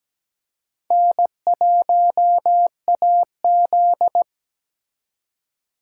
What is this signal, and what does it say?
Morse code, heard as a keyed tone, about 20 wpm, 700 Hz. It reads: N1AZ
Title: Speed Racing - USA 1x2 Call Signs 17wpm